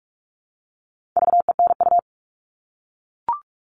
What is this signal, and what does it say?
Morse code, heard as a keyed tone, about 44 wpm, 700 Hz. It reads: VENU
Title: All Cut Numbers - 4-Digit 45wpm